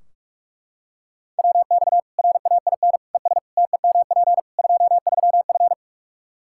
Morse code, heard as a keyed tone, about 41 wpm, 700 Hz. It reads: WX RAIN ES TEMP 23F
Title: K1USN SST QSO Practice 45wpm